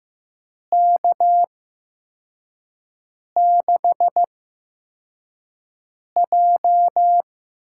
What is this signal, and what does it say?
Morse code, heard as a keyed tone, about 15 wpm, spaced as for 4 wpm, 700 Hz. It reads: K6J
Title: USA Call Signs - 6 Characters - 8x Character Spacing - 15wpm